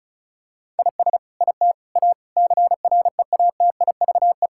Morse code, heard as a keyed tone, about 36 wpm, 700 Hz. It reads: IS IT A CREATIVE
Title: Sentences from Top 1000 Words - Review Part 5 - 35wpm